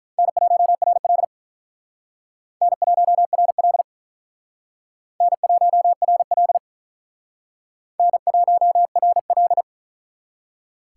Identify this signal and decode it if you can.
Morse code, heard as a keyed tone, about 43 wpm, 700 Hz. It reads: N1RL N1RL N1RL N1RL
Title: Speed Racing - USA 1x2 Call Signs 35wpm